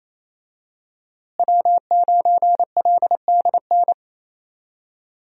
Morse code, heard as a keyed tone, about 28 wpm, 700 Hz. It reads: W9LDN
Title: USA Call Signs - 5 Characters - Repeated 3x - 28wpm